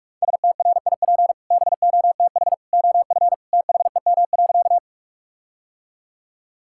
Morse code, heard as a keyed tone, about 46 wpm, 700 Hz. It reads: STRIP BOTH OF THEM.
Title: Sentences from Top 1600 Words 45wpm